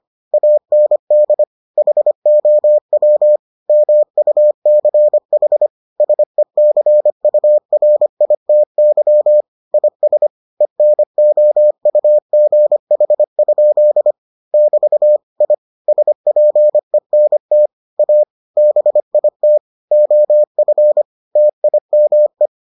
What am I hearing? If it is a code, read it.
Morse code, heard as a keyed tone, about 25 wpm, 600 Hz. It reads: AND HOW MUCH SECURITY IS ENOUGH? <BT> I SPENT A BIT OF TIME